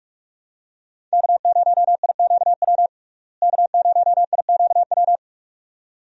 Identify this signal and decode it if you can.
Morse code, heard as a keyed tone, about 45 wpm, 700 Hz. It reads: K0IQW K0IQW
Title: USA Call Signs - 5 Characters - Repeated 2x - 45wpm